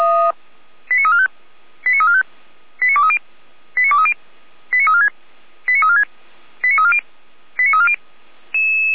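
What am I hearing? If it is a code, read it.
Five-tone selective call (ZVEI), heard as quick runs of five tones, tones of about 70 ms. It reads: 89345 89345 89230 89230 89346 89346 89340 89340
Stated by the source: ZVEI five-tone selective call 89097, received on SDR.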